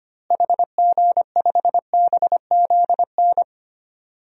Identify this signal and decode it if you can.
Morse code, heard as a keyed tone, about 25 wpm, 700 Hz. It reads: HG5BZN
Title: DXCC-100 Call Signs - 6 Characters - Repeated 3x - 25wpm